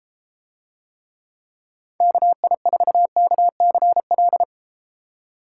Morse code, heard as a keyed tone, about 32 wpm, 700 Hz. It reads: KI4KCL